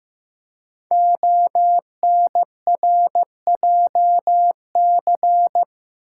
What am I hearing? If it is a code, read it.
Morse code, heard as a keyed tone, about 15 wpm, 700 Hz. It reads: ONRJC
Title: Code Groups - 5 Letters - 15wpm